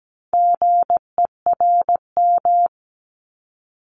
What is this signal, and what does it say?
Morse code, heard as a keyed tone, about 17 wpm, 700 Hz. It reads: GERM
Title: Multi-letter Phonemes 17wpm